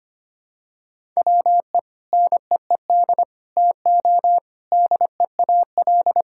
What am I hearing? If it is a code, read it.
Morse code, heard as a keyed tone, about 25 wpm, 700 Hz. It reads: WE NEED TO DEAL
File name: practice / 25wpm / Sentences with 4-Word Phrases - 25wpm